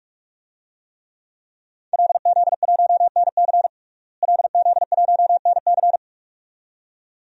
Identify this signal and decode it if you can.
Morse code, heard as a keyed tone, about 45 wpm, 700 Hz. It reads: LZ1NC LZ1NC
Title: DXCC-100 Call Signs - 5 Characters - Repeated 2x - 45wpm